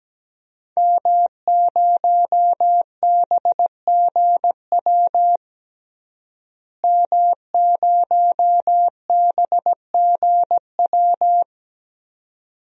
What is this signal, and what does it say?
Morse code, heard as a keyed tone, about 17 wpm, 700 Hz. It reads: M0BGW M0BGW
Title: DXCC-100 Call Signs - 5 Characters - Repeated 2x - 17wpm